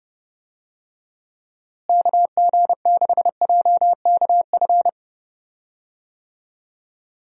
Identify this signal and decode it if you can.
Morse code, heard as a keyed tone, about 30 wpm, 700 Hz. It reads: KG6JKF
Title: USA Call Signs - 6 Characters - Repeated 2x - 30wpm